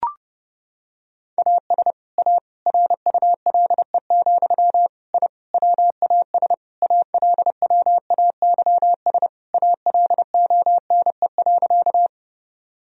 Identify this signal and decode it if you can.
Morse code, heard as a keyed tone, about 30 wpm, 700 Hz. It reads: AS A RULE, I WAS ALWAYS ALONE.
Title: Sentences from Top 1000 Words - Review Part 2 - 30wpm